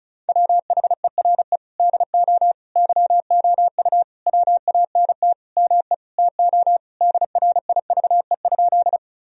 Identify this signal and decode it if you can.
Morse code, heard as a keyed tone, about 36 wpm, 700 Hz. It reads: WHERE DO YOU WANT ME TO DRIVE?
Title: Sentences from Top 1000 Words - Review Part 2 - 35wpm